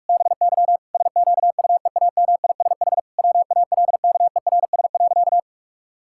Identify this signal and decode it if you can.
Morse code, heard as a keyed tone, about 41 wpm, 700 Hz. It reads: BY SQUEAMISH WALKERS.
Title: Pangram Challenge v2 - 45wpm